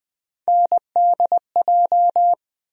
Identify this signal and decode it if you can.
Morse code, heard as a keyed tone, about 20 wpm, 700 Hz. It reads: NDJ
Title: USA Call Sign Postfixes - 3 Letters 20wpm